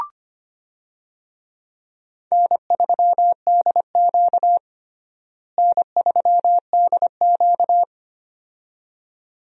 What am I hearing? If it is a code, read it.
Morse code, heard as a keyed tone, about 25 wpm, 700 Hz. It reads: N3DQ N3DQ
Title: USA Call Signs - 4 Characters - Repeated 2x - 25wpm